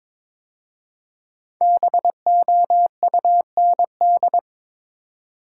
Morse code, heard as a keyed tone, about 22 wpm, 700 Hz. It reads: BOUND
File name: 5 Letter Words - First 3 Letters Given - 22wpm